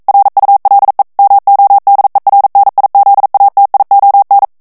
Morse code, heard as a keyed tone, about 42 wpm, 800 Hz. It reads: RULE MODERNIZATION